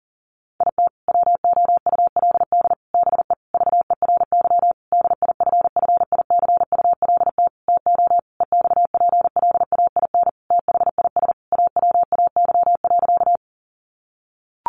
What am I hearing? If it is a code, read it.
Morse code, heard as a keyed tone, about 37 wpm, 700 Hz. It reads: IT WOULD BE VERY DIFFICULT TO EXPLAIN THIS AWAY.
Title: Sentences from Top 1100 Words 40wpm